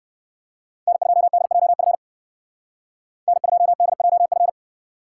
Morse code, heard as a keyed tone, about 58 wpm, 700 Hz. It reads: N2DJF N2DJF